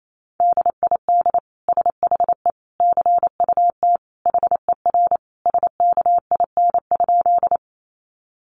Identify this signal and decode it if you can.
Morse code, heard as a keyed tone, about 28 wpm, 700 Hz. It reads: DID SHE CUT HER SKIN?
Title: Sentences from Top 1300 Words 28wpm